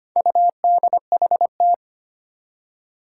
Morse code, heard as a keyed tone, about 25 wpm, 700 Hz. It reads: UDHT
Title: Four Challenging Letter-Number Combinations 25wpm